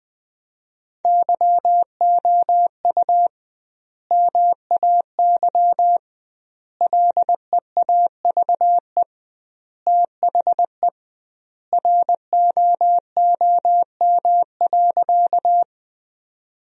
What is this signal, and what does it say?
Morse code, heard as a keyed tone, about 21 wpm, 700 Hz. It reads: YOU MAY LEAVE THE ROOM.
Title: 2x Word-Spacing - Sentences from Top 400 Words - 20wpm